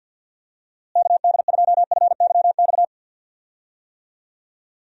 Morse code, heard as a keyed tone, about 51 wpm, 700 Hz. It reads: KD2FYX
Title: USA Call Signs - 6 Characters - Repeated 2x - 50wpm